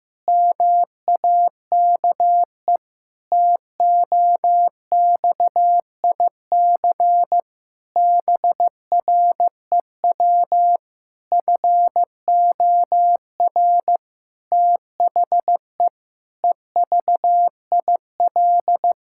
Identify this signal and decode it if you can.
Morse code, heard as a keyed tone, about 16 wpm, 700 Hz. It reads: MAKE TOXIC BREW FOR THE EVIL
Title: Pangram Challenge v2 - 15wpm